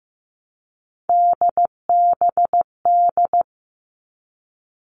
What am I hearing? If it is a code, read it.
Morse code, heard as a keyed tone, about 15 wpm, 700 Hz. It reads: DBD
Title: All Cut Numbers - 3-Digit 15wpm